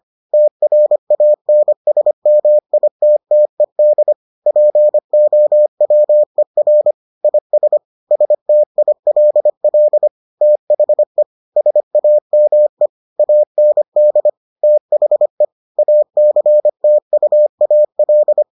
Morse code, heard as a keyed tone, about 25 wpm, 600 Hz. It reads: TRANSMITTED POWER IS STILL THE SAME AND THE ACTUAL